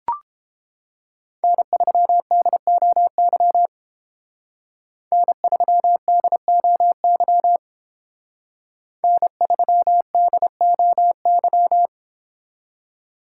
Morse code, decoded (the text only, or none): N3DOY N3DOY N3DOY